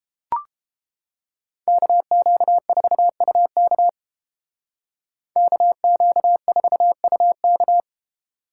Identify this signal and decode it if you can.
Morse code, heard as a keyed tone, about 30 wpm, 700 Hz. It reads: KQ4UK KQ4UK